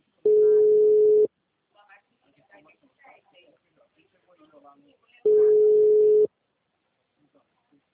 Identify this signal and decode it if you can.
Ringback tone (Europe, 425 Hz), with speech.